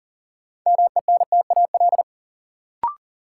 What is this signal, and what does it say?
Morse code, heard as a keyed tone, about 40 wpm, 700 Hz. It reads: MENTAL
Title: Sentences from Top 1000 Words - Review Part 4 - New Word Drill - 40wpm